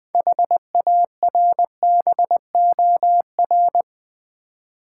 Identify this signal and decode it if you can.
Morse code, heard as a keyed tone, about 21 wpm, 700 Hz. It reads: HARBOR
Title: 6 Letter Words - First 3 Letters Given - 20wpm